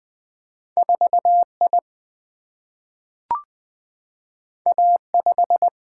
Morse code, heard as a keyed tone, about 20 wpm, 700 Hz. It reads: 4I A5
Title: Single-Letter Single-Number 20wpm